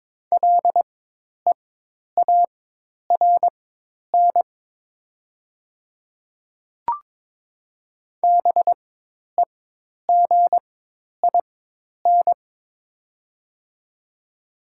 Morse code, heard as a keyed tone, about 23 wpm, spaced as for 10 wpm, 690 Hz. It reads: LEARN BEGIN
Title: 4x Character Spacing - Top 300 Words - 22wpm